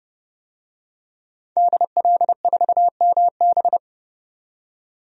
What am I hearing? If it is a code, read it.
Morse code, heard as a keyed tone, about 31 wpm, 700 Hz. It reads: DL4MB
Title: DXCC-100 Call Signs - 5 Characters - 30wpm